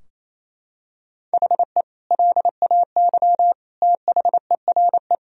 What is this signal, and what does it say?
Morse code, heard as a keyed tone, about 28 wpm, 700 Hz. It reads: HE LAY THERE
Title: Sentences from Top 1000 Words - Review Part 2 - 28wpm